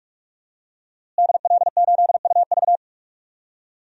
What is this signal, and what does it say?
Morse code, heard as a keyed tone, about 45 wpm, 700 Hz. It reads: DL8UV
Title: DXCC-100 Call Signs - 5 Characters - Repeated 2x - 45wpm